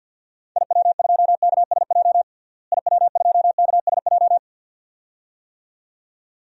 Morse code, heard as a keyed tone, about 47 wpm, 700 Hz. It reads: IW2KSJ IW2KSJ
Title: DXCC-100 Call Signs - 6 Characters - Repeated 2x - 50wpm